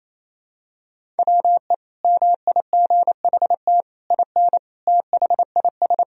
Morse code, heard as a keyed tone, about 28 wpm, 700 Hz. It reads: WE MIGHT IN THIS
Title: Sentences from Top 1000 Words - Review Part 1 - 28wpm